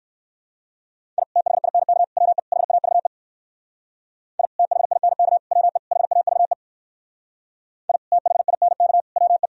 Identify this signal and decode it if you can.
Morse code, heard as a keyed tone, about 58 wpm, 700 Hz. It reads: I THINK WE HAVE I THINK WE HAVE I THINK WE